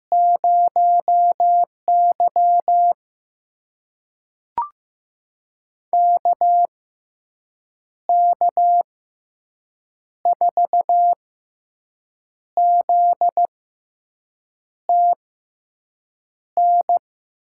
Morse code, heard as a keyed tone, about 15 wpm, spaced as for 6 wpm, 700 Hz. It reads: 0Y KK4ZTN
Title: USA Call Signs - 6x Character Spacing - 15wpm